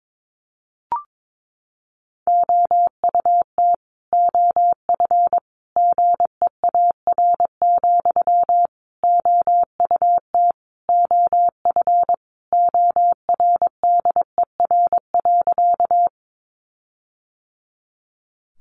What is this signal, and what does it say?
Morse code, heard as a keyed tone, about 22 wpm, 700 Hz. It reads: OUT OF GEAR, OUT OF ORDER.
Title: Sentences from Top 1400 Words 22wpm